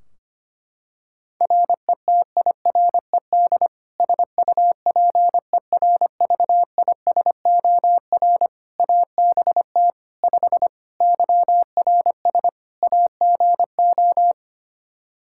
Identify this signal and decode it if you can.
Morse code, heard as a keyed tone, about 25 wpm, 700 Hz. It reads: RETIRED SUPERVISOR ABT 5 YRS AGO
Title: Practice QSO - Exchange 3 - More Info - 25wpm